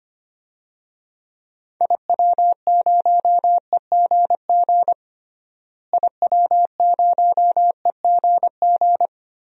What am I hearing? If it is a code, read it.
Morse code, heard as a keyed tone, about 25 wpm, 700 Hz. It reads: IW0EGG IW0EGG
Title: DXCC-100 Call Signs - 6 Characters - Repeated 2x - 25wpm